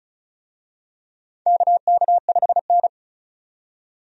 Morse code, heard as a keyed tone, about 34 wpm, 700 Hz. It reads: KK5N
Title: USA Call Signs - 4 Characters - 35wpm